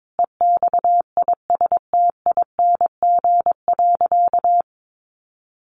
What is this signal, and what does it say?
Morse code, heard as a keyed tone, about 22 wpm, 700 Hz. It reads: EXISTING.
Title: Sentences from Top 1500 Words 22wpm